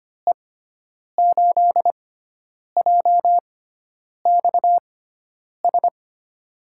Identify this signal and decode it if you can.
Morse code, heard as a keyed tone, about 25 wpm, spaced as for 9 wpm, 700 Hz. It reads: E8JXS
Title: USA Call Signs - 6 Characters - 6x Character Spacing - 25wpm